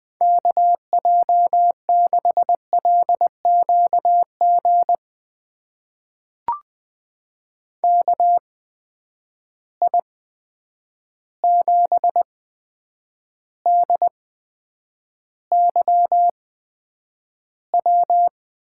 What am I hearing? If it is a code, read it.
Morse code, heard as a keyed tone, about 20 wpm, spaced as for 9 wpm, 700 Hz. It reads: KJ6LQG KI7DYW
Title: USA Call Signs - 6 Characters - 8x Character Spacing - 20wpm